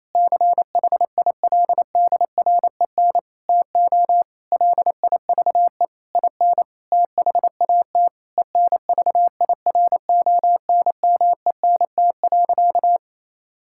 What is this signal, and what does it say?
Morse code, heard as a keyed tone, about 28 wpm, 700 Hz. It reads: CHILDREN TO LIVE IN THAT ENVIRONMENT.
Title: Sentences from Top 1000 Words - Review Part 5 - 28wpm